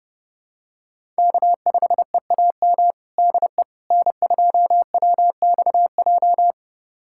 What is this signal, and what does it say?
Morse code, heard as a keyed tone, about 28 wpm, 700 Hz. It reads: K5EAM DE N2WXJ
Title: USA Callsign DE Callsign - 5 Characters - 30wpm